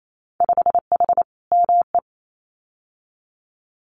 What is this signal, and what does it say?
Morse code, heard as a keyed tone, about 28 wpm, 700 Hz. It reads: 5H ME
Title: Winter Field Day Exchanges Encore 28wpm